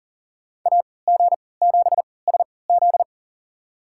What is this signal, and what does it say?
Morse code, heard as a keyed tone, about 38 wpm, spaced as for 22 wpm, 700 Hz. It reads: AG7SZ
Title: USA Call Signs - 5 Characters - 3x Character Spacing - 40wpm